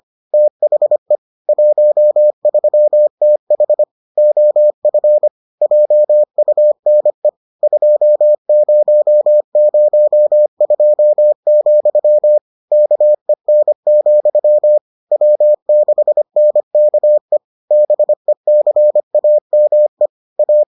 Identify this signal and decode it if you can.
Morse code, heard as a keyed tone, about 25 wpm, 600 Hz. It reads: THE 13TH OF JUNE 2002, KEN, W6NKE BECAME A